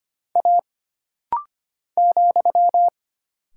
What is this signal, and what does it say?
Morse code, heard as a keyed tone, about 25 wpm, 700 Hz. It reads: A ,
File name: Single Character (Letter:Number:Punctuation:Prosign) - Rapid-Fire - 25wpm